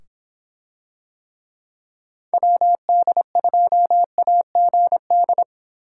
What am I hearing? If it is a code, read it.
Morse code, heard as a keyed tone, about 24 wpm, 700 Hz. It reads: WD2AGD